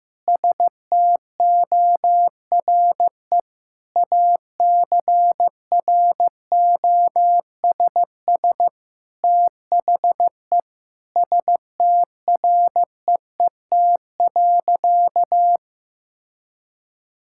Morse code, heard as a keyed tone, about 15 wpm, 700 Hz. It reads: STORE ACROSS THE STREET.
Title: Sentences from Top 1000 Words - Review Part 5 - 15wpm